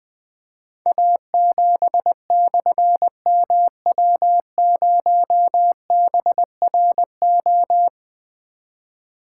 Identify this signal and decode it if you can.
Morse code, heard as a keyed tone, about 20 wpm, 700 Hz. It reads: A7/MW0BRO
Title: Difficult Call Signs 20wpm